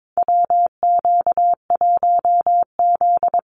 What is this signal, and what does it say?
Morse code, heard as a keyed tone, about 22 wpm, 700 Hz. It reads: WQ1Z
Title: USA Call Signs Repeated 2x 22wpm